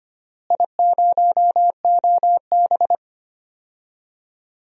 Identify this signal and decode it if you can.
Morse code, heard as a keyed tone, about 25 wpm, 700 Hz. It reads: I0OB
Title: DXCC-100 Call Signs - 4 Characters - Repeated 3x - 25wpm